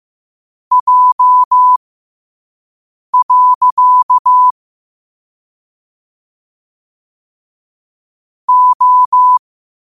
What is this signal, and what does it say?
Morse code, heard as a keyed tone, about 15 wpm, spaced as for 5 wpm, 1000 Hz. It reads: J. O
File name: qotdfiles / 5wpm